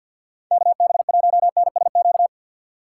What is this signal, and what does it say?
Morse code, heard as a keyed tone, about 51 wpm, 700 Hz. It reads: KB1NSQ
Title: USA Call Signs - 6 Characters - 3x Character Spacing - 50wpm